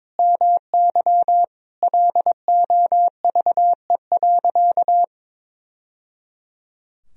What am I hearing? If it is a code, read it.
Morse code, heard as a keyed tone, about 22 wpm, 700 Hz. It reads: MY LOVE.
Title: Sentences from Top 1000 Words - Review Part 2 - 22wpm